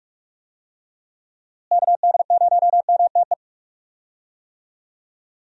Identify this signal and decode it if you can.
Morse code, heard as a keyed tone, about 45 wpm, 700 Hz. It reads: KD0MTE
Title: USA Call Signs - 6 Characters - Repeated 2x - 45wpm